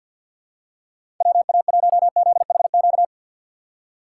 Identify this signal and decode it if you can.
Morse code, heard as a keyed tone, about 52 wpm, 690 Hz. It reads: WA1ZHQ
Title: USA Call Signs - 6 Characters - 3x Character Spacing - 50wpm